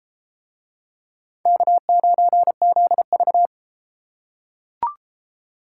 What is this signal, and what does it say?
Morse code, heard as a keyed tone, about 30 wpm, 700 Hz. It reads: K9ZV